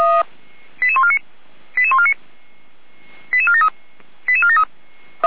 Five-tone selective call (ZVEI), heard as quick runs of five tones, tones of about 70 ms. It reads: 88149 88149 88473 88473